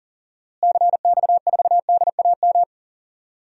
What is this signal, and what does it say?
Morse code, heard as a keyed tone, about 40 wpm, 700 Hz. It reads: CX4DAM
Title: DXCC-100 Call Signs - 6 Characters - Repeated 2x - 40wpm